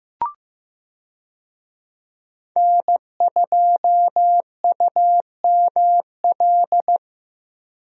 Morse code, heard as a keyed tone, about 15 wpm, 700 Hz. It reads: N2UML
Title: USA Call Signs - 5 Characters - Repeated 3x - 15wpm